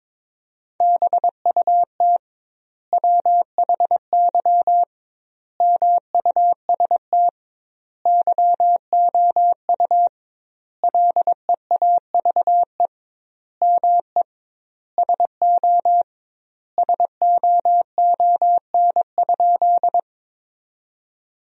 Morse code, heard as a keyed tone, about 21 wpm, 700 Hz. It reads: BUT WHY MUST YOU LEAVE ME SO SOON?
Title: 2x Word-Spacing - Sentences from Top 400 Words - 22wpm